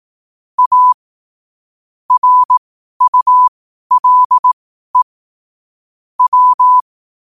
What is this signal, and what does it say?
Morse code, heard as a keyed tone, about 18 wpm, spaced as for 12 wpm, 1000 Hz. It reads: A RULE W
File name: qotdfiles / 12wpm